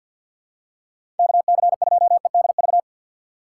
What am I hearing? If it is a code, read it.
Morse code, heard as a keyed tone, about 50 wpm, 700 Hz. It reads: KC2EDV
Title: USA Call Signs - 6 Characters - 3x Character Spacing - 50wpm